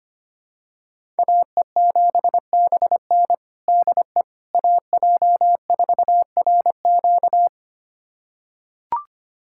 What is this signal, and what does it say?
Morse code, heard as a keyed tone, about 25 wpm, 700 Hz. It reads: AE7BN DE AJ4RQ
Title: USA Callsign DE Callsign - 5 Characters - 25wpm